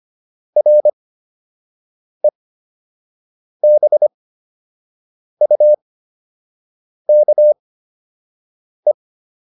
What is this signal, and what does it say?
Morse code, heard as a keyed tone, about 25 wpm, spaced as for 6 wpm, 600 Hz. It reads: REBUKE